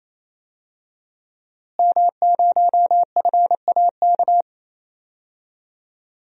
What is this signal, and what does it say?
Morse code, heard as a keyed tone, about 28 wpm, 700 Hz. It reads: M0FAK
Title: DXCC-100 Call Signs - 5 Characters - Repeated 3x - 28wpm